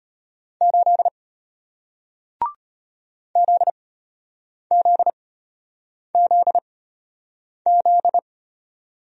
Morse code, heard as a keyed tone, about 34 wpm, spaced as for 8 wpm, 700 Hz. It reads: 8 ZZZZ